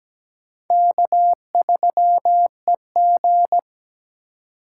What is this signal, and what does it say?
Morse code, heard as a keyed tone, about 20 wpm, 700 Hz. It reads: K3EG
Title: Speed Racing - USA Call Signs 17wpm